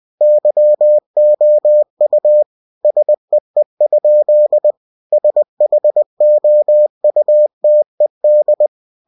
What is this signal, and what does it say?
Morse code, heard as a keyed tone, about 20 wpm, 600 Hz. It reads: YOU SEE? SHOUTED
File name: cw-books / winnie-the-pooh / winnie-20wpm-page-0072